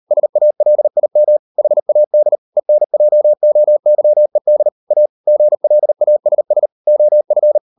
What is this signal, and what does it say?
Morse code, heard as a keyed tone, about 38 wpm, 600 Hz. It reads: SALIM HAD ENJOYED A GLASS OF